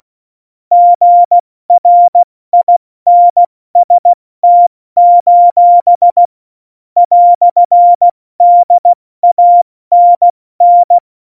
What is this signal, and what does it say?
Morse code, heard as a keyed tone, about 16 wpm, 700 Hz. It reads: GRINST: "DANN
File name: Morsekurs / Witze / Witz 22. Dezember 2020, 70 F80